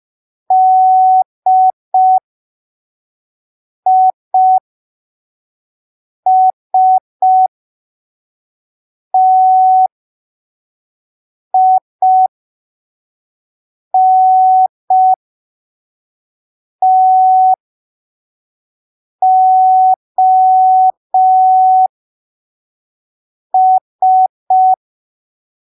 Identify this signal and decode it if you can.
Morse code, heard as a keyed tone, about 5 wpm, spaced as for 3 wpm, 750 Hz. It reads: DISTINTOS
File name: arquivos / texto / texto 22 05ppm